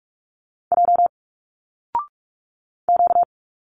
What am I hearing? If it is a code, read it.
Morse code, heard as a keyed tone, about 45 wpm, 700 Hz. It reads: J Q
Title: Single Letters 45wpm